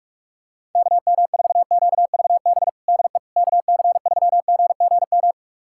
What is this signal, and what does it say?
Morse code, heard as a keyed tone, about 46 wpm, 700 Hz. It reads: KM4QVB DE KC3GGM
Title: USA Callsign DE Callsign - 6 Characters - 45wpm